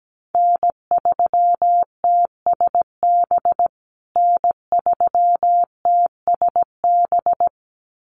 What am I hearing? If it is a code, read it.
Morse code, heard as a keyed tone, about 17 wpm, 700 Hz. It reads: N3TSB N3TSB
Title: USA Call Signs Repeated 2x 17wpm